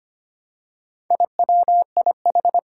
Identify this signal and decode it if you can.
Morse code, heard as a keyed tone, about 25 wpm, 700 Hz. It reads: IWIH
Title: Four Challenging Letter-Number Combinations 25wpm